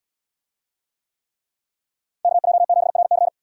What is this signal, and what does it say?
Morse code, heard as a keyed tone, about 77 wpm, 700 Hz. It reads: KJ7RC